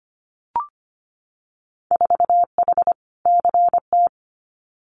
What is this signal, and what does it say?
Morse code, heard as a keyed tone, about 25 wpm, 700 Hz. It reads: 4H CT
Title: Winter Field Day Exchanges Encore 25wpm